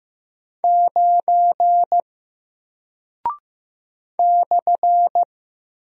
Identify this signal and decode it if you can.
Morse code, heard as a keyed tone, about 15 wpm, 700 Hz. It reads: 9 /
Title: Single Character (Letter:Number:Punctuation:Prosign) Alt 15wpm